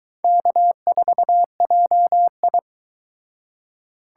Morse code, heard as a keyed tone, about 22 wpm, 700 Hz. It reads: K4JI